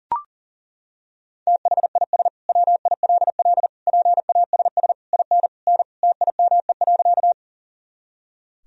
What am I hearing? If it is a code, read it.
Morse code, heard as a keyed tone, about 37 wpm, 700 Hz. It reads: THIS WILL PASS IN N TIME.
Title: Sentences from Top 400 Words and Abbreviations 40wpm